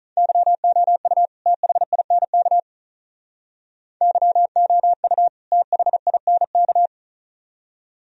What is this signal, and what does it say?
Morse code, heard as a keyed tone, about 43 wpm, 700 Hz. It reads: YOU THINK YOU THINK
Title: Speed-Racing - Sets of 2 Words 35wpm